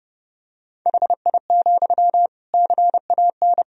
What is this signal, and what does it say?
Morse code, heard as a keyed tone, about 30 wpm, 700 Hz. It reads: HI, CAN
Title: Sentences from Top 1000 Words - Review Part 3 - 30wpm